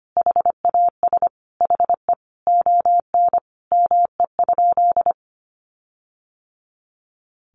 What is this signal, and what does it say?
Morse code, heard as a keyed tone, about 25 wpm, 700 Hz. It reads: HAS HE ON ME?
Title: Sentences from Top 1500 Words 25wpm